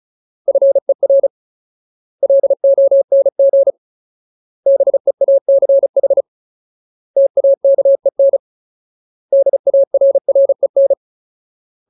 Morse code, heard as a keyed tone, about 34 wpm, 550 Hz. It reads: FER LONG BEACH TAKEN DARREN
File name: Ultimate Word List / 35WPM Ultimate